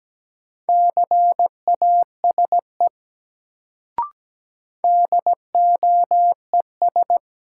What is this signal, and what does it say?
Morse code, heard as a keyed tone, about 17 wpm, 700 Hz. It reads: CASE DOES
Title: Sets of 2 Words - New Word Drill - 17wpm